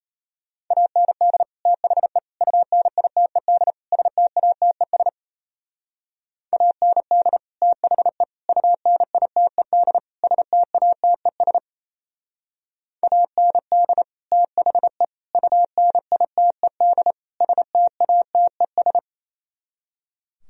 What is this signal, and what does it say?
Morse code, heard as a keyed tone, about 37 wpm, 700 Hz. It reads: AND THE UNITED STATES AND THE UNITED STATES AND THE UNITED STATES